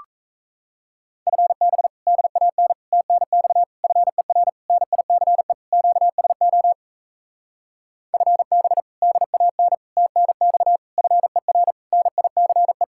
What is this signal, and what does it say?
Morse code, heard as a keyed tone, about 37 wpm, 700 Hz. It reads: FB DAN TNX FER NICE QSO FB DAN TNX FER NICE